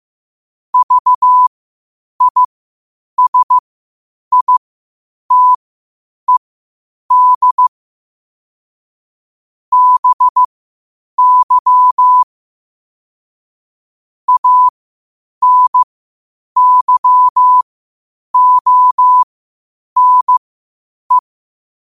Morse code, heard as a keyed tone, about 15 wpm, spaced as for 8 wpm, 1000 Hz. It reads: VISITED BY ANYONE
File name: 8wpm